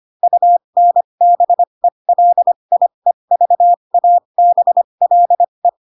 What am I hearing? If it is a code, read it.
Morse code, heard as a keyed tone, about 25 wpm, spaced as for 22 wpm, 700 Hz. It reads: UNBELIEVABLE
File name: Pinocchio / Pinocchio ch3 18 25 ik1hge